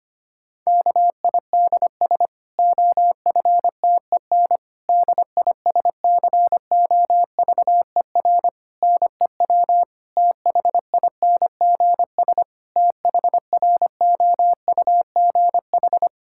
Morse code, heard as a keyed tone, about 25 wpm, 700 Hz. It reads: KIDS OFTEN DISCOVER NEW THINGS THROUGH
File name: practice / 25wpm / Difficult Sentences with Hints - 25wpm